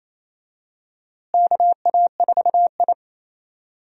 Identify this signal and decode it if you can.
Morse code, heard as a keyed tone, about 28 wpm, 700 Hz. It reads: KA4I
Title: USA Call Signs - 4 Characters - Repeated 3x - 28wpm